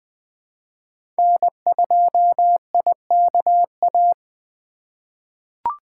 Morse code, heard as a keyed tone, about 20 wpm, 700 Hz. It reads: N2IKA
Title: USA Call Signs - 5 Characters - 2x Character Spacing - 20wpm